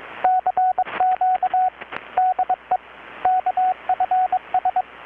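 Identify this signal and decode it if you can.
Morse code, heard as a keyed tone, about 22 wpm, 740 Hz, with radio noise. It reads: CQ DE KFS
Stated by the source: Morse from KPH's Night of Nights XX maritime radiotelegraphy event, received in Chiapas, Mexico about 3,800 km away on a Kenwood R-600 with a grounded 10 m long-wire antenna.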